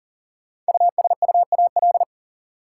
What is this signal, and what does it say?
Morse code, heard as a keyed tone, about 40 wpm, 700 Hz. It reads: USUAL
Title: Sentences from Top 400 Words - New Word Drill - 40wpm